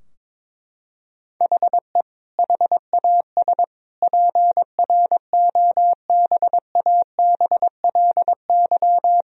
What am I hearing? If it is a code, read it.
Morse code, heard as a keyed tone, about 22 wpm, 700 Hz. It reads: HE HAS PROBABLY
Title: Sentences from Top 1000 Words - Review Part 4 - 22wpm